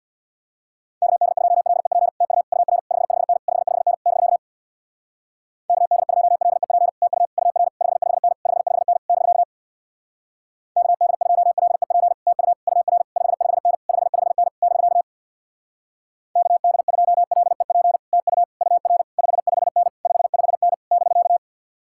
Morse code, heard as a keyed tone, about 64 wpm, 700 Hz. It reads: KD2LEP TU UR 55N 55N <BK> KD2LEP TU UR 55N 55N <BK> KD2LEP TU UR 55N 55N <BK> KD2LEP TU UR 55N 55N <BK>